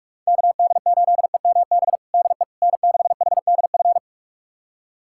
Morse code, heard as a keyed tone, about 46 wpm, 700 Hz. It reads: KD8EMB DE N6HDF
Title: USA Callsign DE Callsign 45wpm